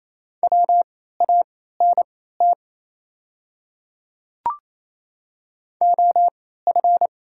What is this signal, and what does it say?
Morse code, heard as a keyed tone, about 27 wpm, spaced as for 16 wpm, 700 Hz. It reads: WANT OF
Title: 3x Character Spacing - Top 300 Words - 28wpm